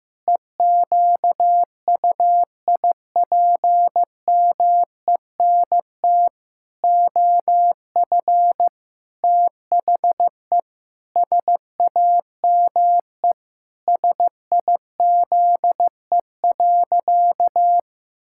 Morse code, heard as a keyed tone, about 15 wpm, 700 Hz. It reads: EQUIPMENT OF THE SAME SIZE.
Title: Pangram Challenge v2 - 15wpm